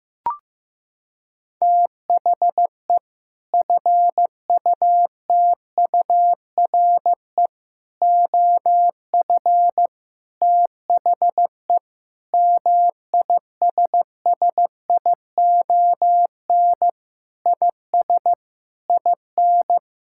Morse code, heard as a keyed tone, about 15 wpm, 700 Hz. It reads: THE FUTURE OF THE MISSION IS IN
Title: Sentences from Top 1600 Words 15wpm